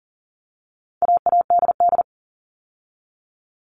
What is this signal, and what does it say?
Morse code, heard as a keyed tone, about 40 wpm, 700 Hz. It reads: AADD
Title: All Cut Numbers - 4-Digit 40wpm